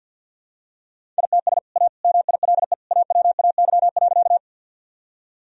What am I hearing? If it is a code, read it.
Morse code, heard as a keyed tone, about 50 wpm, 700 Hz. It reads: ITS A MILE AWAY.
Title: Sentences from Top 1000 Words - Review Part 1 - 50wpm